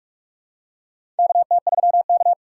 Morse code, heard as a keyed tone, about 44 wpm, 690 Hz. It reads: KT3K
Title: USA Call Signs - 4 Characters - 3x Character Spacing - 45wpm